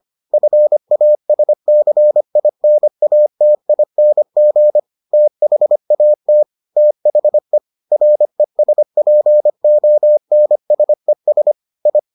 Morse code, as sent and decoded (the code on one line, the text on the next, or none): ..-. .- ... -.-. .. -. .- - .. -. --. / - .... .- - / - .... / .-. ... .--. --- -. ... ... / ..
FASCINATING THAT THE RESPONSES I